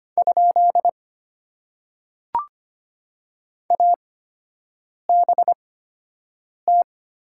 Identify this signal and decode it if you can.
Morse code, heard as a keyed tone, about 25 wpm, spaced as for 7 wpm, 700 Hz. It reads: ? ABT
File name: Top 90 QSO Elements - 8x Character Spacing - 25wpm